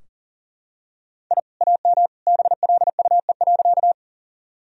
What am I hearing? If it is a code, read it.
Morse code, heard as a keyed tone, about 40 wpm, 700 Hz. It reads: I AM BLUE.
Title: Sentences from Top 1000 Words - Review Part 3 - 40wpm